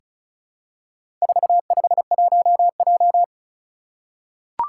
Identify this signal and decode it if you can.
Morse code, heard as a keyed tone, about 36 wpm, 700 Hz. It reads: 451J
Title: Four Challenging Letter-Number Combinations 35wpm